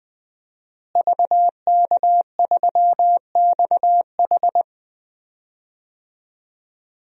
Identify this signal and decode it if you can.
Morse code, heard as a keyed tone, about 21 wpm, 700 Hz. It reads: VK3XH
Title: DXCC-100 Call Signs - 5 Characters - Repeated 3x - 20wpm